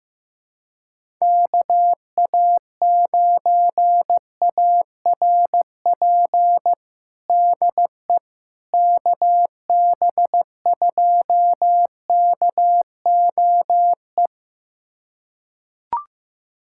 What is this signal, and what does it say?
Morse code, heard as a keyed tone, about 15 wpm, 700 Hz. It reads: KA9ARP DE KB2KOE
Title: USA Callsign DE Callsign - 6 Characters - 15wpm